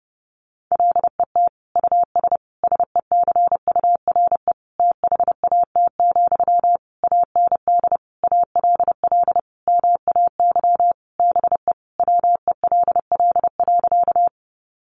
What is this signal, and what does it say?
Morse code, heard as a keyed tone, about 30 wpm, 700 Hz. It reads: LET US SECURE THAT, AND ALL MAY BE WELL.
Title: Sentences from Top 1200 Words 30wpm